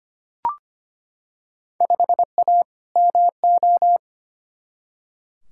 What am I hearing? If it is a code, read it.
Morse code, heard as a keyed tone, about 25 wpm, 700 Hz. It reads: 5A MO
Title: Field Day Exchanges Encore - 25wpm